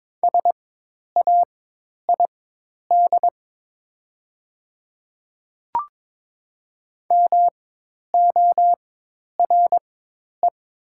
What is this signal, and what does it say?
Morse code, heard as a keyed tone, about 22 wpm, spaced as for 10 wpm, 700 Hz. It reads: SAID MORE
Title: 4x Character Spacing - Top 200 Words - 22wpm